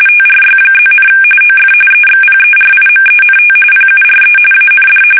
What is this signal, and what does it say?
RTTY, about 45.45 baud. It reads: TRY+, OF EXPERIMENT PURE AND
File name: fsk850 / benjamin-g-lamme